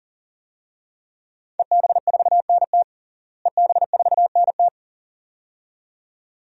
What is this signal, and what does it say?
Morse code, heard as a keyed tone, about 40 wpm, 700 Hz. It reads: EB4NT EB4NT
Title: DXCC-100 Call Signs - 5 Characters - Repeated 2x - 40wpm